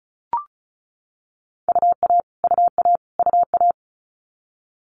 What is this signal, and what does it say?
Morse code, heard as a keyed tone, about 36 wpm, 700 Hz. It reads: UA UA UA
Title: Multi-letter Phonemes Repeated 3x 35wpm